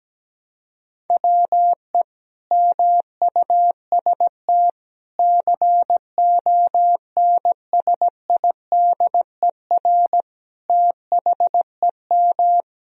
Morse code, heard as a keyed tone, about 17 wpm, 700 Hz. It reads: WE MUST CONSIDER THEM
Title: Sentences from Top 1000 Words - Review Part 4 - 17wpm